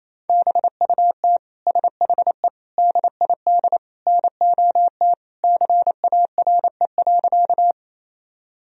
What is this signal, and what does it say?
Morse code, heard as a keyed tone, about 28 wpm, 700 Hz. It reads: BUT SHE DID NOT CARE.
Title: Sentences from Top 1000 Words - Review Part 1 - 28wpm